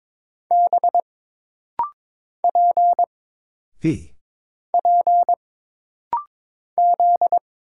Morse code, heard as a keyed tone, about 22 wpm, 700 Hz. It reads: B P P Z
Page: This rapid-fire timing is geared to push your ICR (Instant Character Recognition) skills.